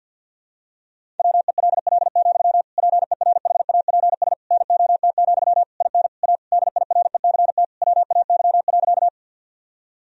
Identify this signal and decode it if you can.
Morse code, heard as a keyed tone, about 50 wpm, 700 Hz. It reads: WELL, PERHAPS NOT, IN A DIRECT WAY.
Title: Sentences from Top 1000 Words - Review Part 2 - 50wpm